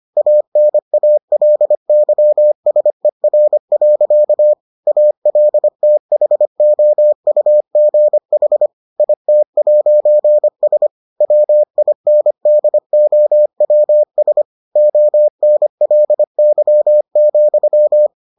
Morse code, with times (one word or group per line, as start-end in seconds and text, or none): 0.2-4.5 ANALYSER.
4.9-8.7 ALTHOUGH
9.0-10.9 IT'S
11.2-14.4 WINDOWS
14.8-18.1 ONLY,